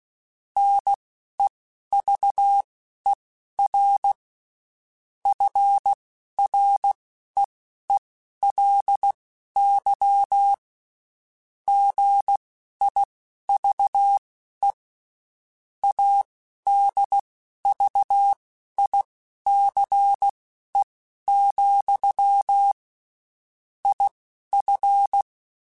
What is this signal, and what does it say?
Morse code, heard as a keyed tone, about 16 wpm, spaced as for 11 wpm, 800 Hz. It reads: NEVER FREELY GIVE ADVICE, IF
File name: Quotes Spaced / Quotes Spaced-mp3s / Advice sp @15